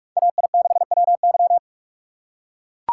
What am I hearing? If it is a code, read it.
Morse code, heard as a keyed tone, about 44 wpm, 700 Hz. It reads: AI6WY